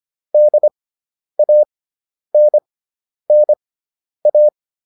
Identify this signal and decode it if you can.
Morse code, heard as a keyed tone, about 25 wpm, spaced as for 10 wpm, 600 Hz. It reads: DANNA